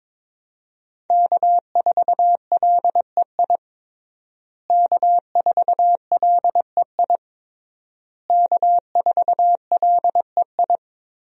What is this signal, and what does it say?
Morse code, heard as a keyed tone, about 22 wpm, 700 Hz. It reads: K4LEI K4LEI K4LEI
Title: USA Call Signs - 5 Characters - Repeated 3x - 22wpm